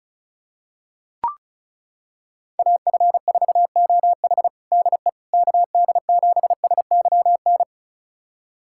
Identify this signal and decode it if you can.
Morse code, heard as a keyed tone, about 36 wpm, 700 Hz. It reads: AF4OH DE KD7SYN
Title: USA Callsign DE Callsign 35wpm